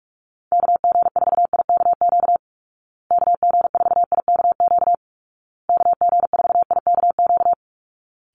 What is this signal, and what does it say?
Morse code, heard as a keyed tone, about 44 wpm, 700 Hz. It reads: KG4IKQ KG4IKQ KG4IKQ
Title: Call Signs Repeated 2x - 45wpm